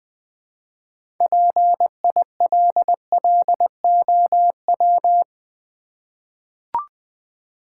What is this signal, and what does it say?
Morse code, heard as a keyed tone, about 20 wpm, 700 Hz. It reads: PILLOW
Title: Easy Words with Hints - 20wpm